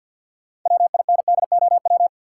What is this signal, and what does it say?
Morse code, heard as a keyed tone, about 52 wpm, 700 Hz. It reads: WINDOW
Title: Sentences from Top 1600 Words - New Word Drill - 50wpm